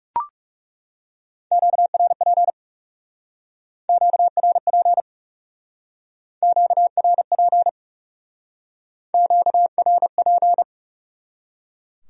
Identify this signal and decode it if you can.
Morse code, heard as a keyed tone, about 39 wpm, 700 Hz. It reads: QRP QRP QRP QRP